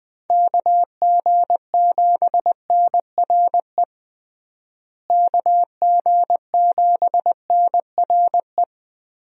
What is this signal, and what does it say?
Morse code, heard as a keyed tone, about 21 wpm, 700 Hz. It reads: KG7NRE KG7NRE
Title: USA Call Signs - 6 Characters - Repeated 3x - 20wpm